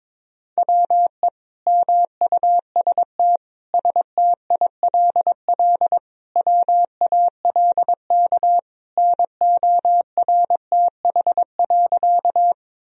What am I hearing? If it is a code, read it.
Morse code, heard as a keyed tone, about 22 wpm, 700 Hz. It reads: WE MUST STILL WALK NORTH.
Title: Sentences from Top 1000 Words - Review Part 1 - 22wpm